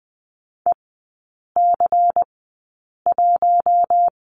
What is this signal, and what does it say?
Morse code, heard as a keyed tone, about 20 wpm, spaced as for 8 wpm, 690 Hz. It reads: EC1
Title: 145 Lesson 24e - Letter y - Three Characters 20wpm